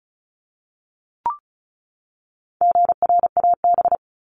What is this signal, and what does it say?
Morse code, heard as a keyed tone, about 34 wpm, 700 Hz. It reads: GRAB
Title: Top 1751-2000 Words 35wpm